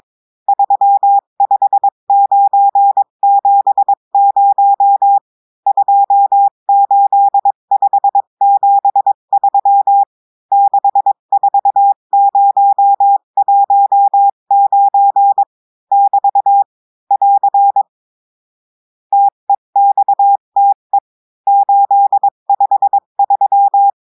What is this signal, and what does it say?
Morse code, heard as a keyed tone, about 22 wpm, 800 Hz. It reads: 35970 28573 64019 <BT> <AR> TEXTE 853
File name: cours-du-mercredi / audios / cours-853-du-2-mai-2024-etxte chiffre et texte clair-a-20-wpm